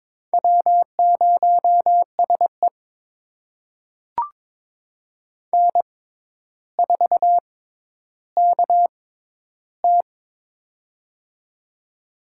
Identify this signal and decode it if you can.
Morse code, heard as a keyed tone, about 22 wpm, spaced as for 11 wpm, 700 Hz. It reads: W0SE N4KT
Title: USA Call Signs - 4 Characters - 6x Character Spacing - 22wpm